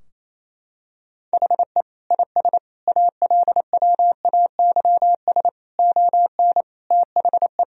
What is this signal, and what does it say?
Morse code, heard as a keyed tone, about 28 wpm, 700 Hz. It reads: HE IS ALWAYS ON THE
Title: Sentences from Top 1000 Words - Review Part 3 - 28wpm